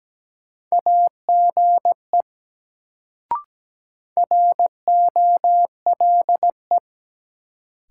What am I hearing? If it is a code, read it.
Morse code, heard as a keyed tone, about 17 wpm, 700 Hz. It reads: AGE ROLE
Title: Sets of 3 Words - New Word Drill - 17wpm